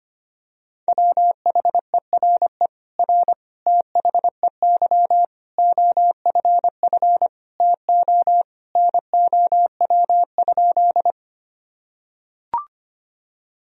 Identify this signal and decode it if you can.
Morse code, heard as a keyed tone, about 25 wpm, 700 Hz. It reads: WHERE R THEY OFF TO NOW?
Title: Sentences from Top 200 Words and Abbreviations 25wpm